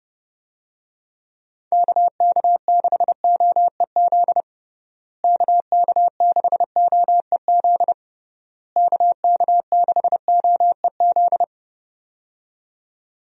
Morse code, heard as a keyed tone, about 30 wpm, 700 Hz. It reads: KK6OEZ KK6OEZ KK6OEZ
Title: USA Call Signs - 6 Characters - Repeated 3x - 30wpm